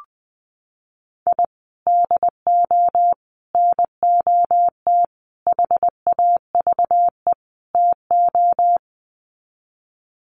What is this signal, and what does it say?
Morse code, heard as a keyed tone, about 19 wpm, 700 Hz. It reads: I DO NOT HAVE TO
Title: Sets of 5 Words 20wpm